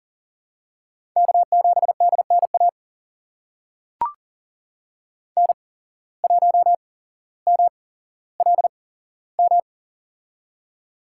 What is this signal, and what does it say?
Morse code, heard as a keyed tone, about 40 wpm, spaced as for 17 wpm, 700 Hz. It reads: K7DNA N1MLM
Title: USA Call Signs - 5 Characters - 8x Character Spacing - 40wpm